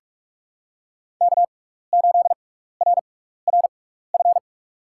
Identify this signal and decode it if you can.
Morse code, heard as a keyed tone, about 44 wpm, spaced as for 15 wpm, 700 Hz. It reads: K8RRF
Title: USA Call Signs - 5 Characters - 6x Character Spacing - 45wpm